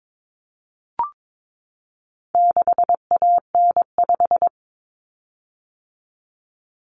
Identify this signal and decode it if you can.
Morse code, heard as a keyed tone, about 22 wpm, 700 Hz. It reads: 6AN5
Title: Common Cut Numbers - 4-Digit 22wpm